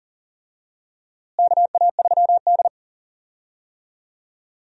Morse code, heard as a keyed tone, about 40 wpm, 700 Hz. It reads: KA3D
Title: USA Call Signs - 4 Characters - Repeated 2x - 40wpm